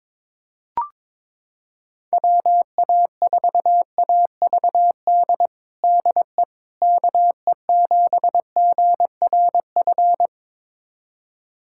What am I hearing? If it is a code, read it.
Morse code, heard as a keyed tone, about 22 wpm, 700 Hz. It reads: WA4AVD DE KE7GRF
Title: USA Callsign DE Callsign 22wpm